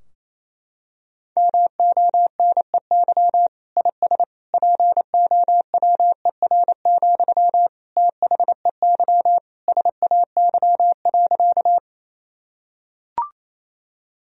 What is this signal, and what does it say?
Morse code, heard as a keyed tone, about 28 wpm, 700 Hz. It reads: MONEY IS POWER, THEY SAY.
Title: Sentences from Top 1000 Words - Review Part 2 - 28wpm